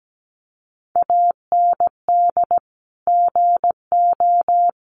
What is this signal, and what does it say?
Morse code, heard as a keyed tone, about 17 wpm, 700 Hz. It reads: AND GO
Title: Sentences from Top 700 Words 17wpm